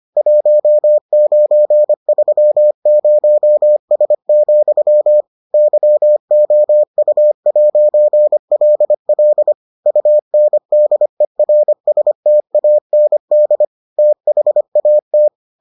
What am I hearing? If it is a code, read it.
Morse code, heard as a keyed tone, about 25 wpm, 600 Hz. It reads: 1930S, YOU'LL UNDERSTAND THAT